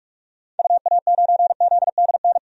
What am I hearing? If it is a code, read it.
Morse code, heard as a keyed tone, about 45 wpm, 700 Hz. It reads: UA9ZDN
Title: DXCC-100 Call Signs - 6 Characters - 45wpm